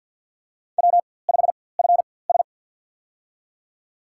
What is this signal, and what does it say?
Morse code, heard as a keyed tone, about 46 wpm, spaced as for 23 wpm, 700 Hz. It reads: W5FS
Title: USA Call Signs - 4x Character Spacing - 50wpm